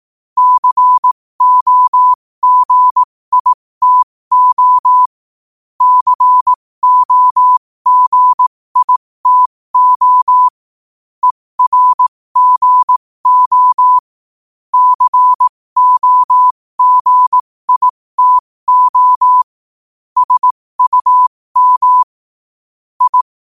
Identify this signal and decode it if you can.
Morse code, heard as a keyed tone, about 18 wpm, spaced as for 15 wpm, 1000 Hz. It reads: COGITO COGITO ERGO COGITO SUM I
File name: qotdfiles / 15wpm